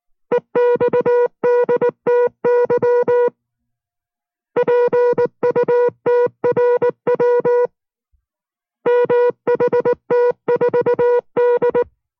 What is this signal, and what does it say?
Morse code, heard as a keyed tone, about 19 wpm, 480 Hz. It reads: EXDTY PUTRW MHT4D